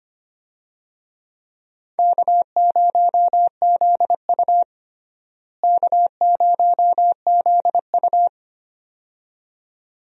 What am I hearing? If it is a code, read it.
Morse code, heard as a keyed tone, about 25 wpm, 700 Hz. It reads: K0ZU K0ZU
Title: USA Call Signs - 4 Characters - Repeated 2x - 25wpm